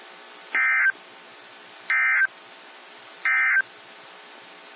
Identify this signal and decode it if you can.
Emergency Alert System (SAME) tones, heard as harsh buzzing bursts. It is heard from 0.4 to 3.7 s.
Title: NOAA Weather Radio - WXJ73